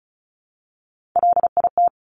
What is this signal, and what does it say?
Morse code, heard as a keyed tone, about 36 wpm, 690 Hz. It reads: LIT
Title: Top Three Letter Combinations in Words 35wpm